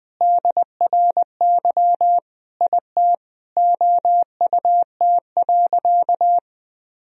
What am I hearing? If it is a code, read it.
Morse code, heard as a keyed tone, about 20 wpm, 700 Hz. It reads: DRY IT OUT.
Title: Sentences from Top 1000 Words - Review Part 3 - 20wpm